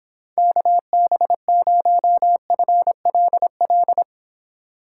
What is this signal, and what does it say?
Morse code, heard as a keyed tone, about 26 wpm, 700 Hz. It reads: KB0FLL